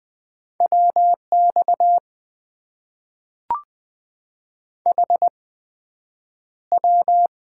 Morse code, heard as a keyed tone, about 20 wpm, 700 Hz. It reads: W X H W
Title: Top 90 QSO Elements - 8x Character Spacing - 20wpm